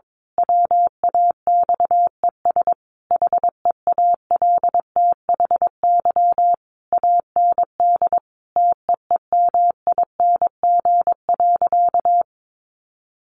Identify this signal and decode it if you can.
Morse code, heard as a keyed tone, about 22 wpm, 700 Hz. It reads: WAXES HEALTHY AND TEEMING.
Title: A Sound Of Thunder 22wpm